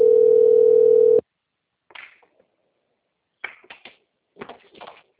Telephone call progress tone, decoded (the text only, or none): ringback tone